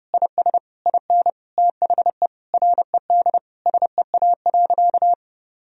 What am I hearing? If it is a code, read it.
Morse code, heard as a keyed tone, about 30 wpm, 700 Hz. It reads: IS IN THE RED SEA.
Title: Sentences from Top 1000 Words - Review Part 2 - 30wpm